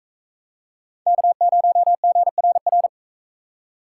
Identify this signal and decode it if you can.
Morse code, heard as a keyed tone, about 42 wpm, 700 Hz. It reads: K0GRR